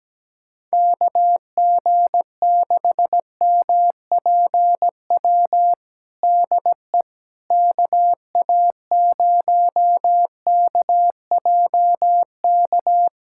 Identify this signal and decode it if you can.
Morse code, heard as a keyed tone, about 17 wpm, 700 Hz. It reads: KG6MPW DE KA0KJK
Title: USA Callsign DE Callsign - 6 Characters - 17wpm